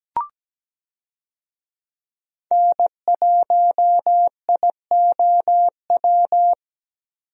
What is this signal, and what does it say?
Morse code, heard as a keyed tone, about 17 wpm, 700 Hz. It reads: N1IOW
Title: USA Call Signs - 5 Characters - 17wpm